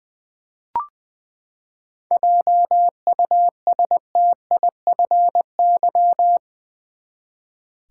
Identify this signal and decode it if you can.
Morse code, heard as a keyed tone, about 20 wpm, 700 Hz. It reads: JUSTIFY
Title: Seven Letter Words - 20wpm